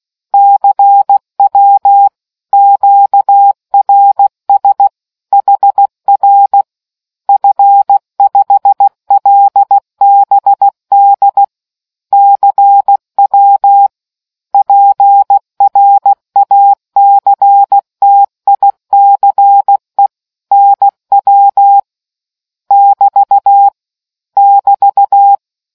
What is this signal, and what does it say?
Morse code, heard as a keyed tone, about 16 wpm, 800 Hz. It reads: CW QRS HR F5LBD CW PRACTICE NW <BT> <BT>